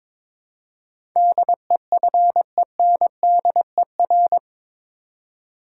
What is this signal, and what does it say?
Morse code, heard as a keyed tone, about 21 wpm, 700 Hz. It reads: DEFENDER
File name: practice / 22wpm / Eight Letter Words - 22wpm